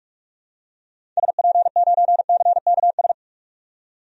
Morse code, heard as a keyed tone, about 46 wpm, 700 Hz. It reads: SP9CKS
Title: DXCC-100 Call Signs - 6 Characters - 45wpm